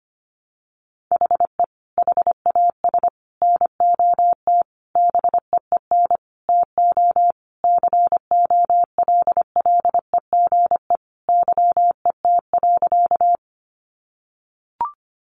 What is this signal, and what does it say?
Morse code, heard as a keyed tone, about 25 wpm, 700 Hz. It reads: HE HAS NOT BEEN TO COLLEGE YET.
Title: Sentences from Top 1300 Words 25wpm